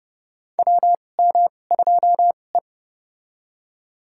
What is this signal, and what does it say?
Morse code, heard as a keyed tone, about 30 wpm, spaced as for 22 wpm, 700 Hz. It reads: WM2E
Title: USA Call Signs - 4 Characters - 2x Character Spacing - 30wpm